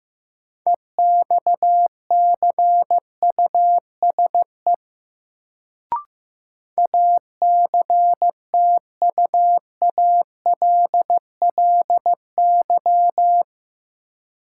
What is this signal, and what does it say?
Morse code, heard as a keyed tone, about 15 wpm, 700 Hz. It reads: EXCUSE ACTUALLY
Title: Sentences from Top 700 Words - New Word Drill - 15wpm